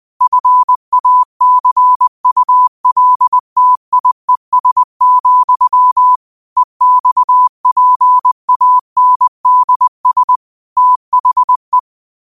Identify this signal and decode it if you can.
Morse code, heard as a keyed tone, about 20 wpm, 1000 Hz. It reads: FACULTIES, EXPANDS THE
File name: qotdfiles / 20wpm